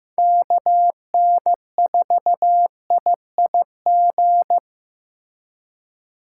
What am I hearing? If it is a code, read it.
Morse code, heard as a keyed tone, about 18 wpm, 700 Hz. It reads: KN4IIG
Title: Speed Racing - USA 2x3 Call Signs 15wpm